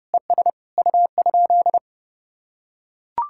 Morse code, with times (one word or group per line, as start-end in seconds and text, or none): 0.1-0.5 ES
0.8-1.8 U?